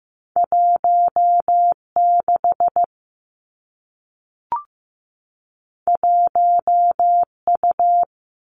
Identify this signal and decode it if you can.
Morse code, heard as a keyed tone, about 15 wpm, 700 Hz. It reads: 1 6 1 U
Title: Two Challenging Letter-Number Combinations 15wpm